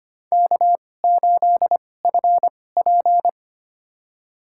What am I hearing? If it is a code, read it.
Morse code, heard as a keyed tone, about 25 wpm, spaced as for 18 wpm, 700 Hz. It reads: K8FP
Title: USA Call Signs - 2x Character Spacing - 25wpm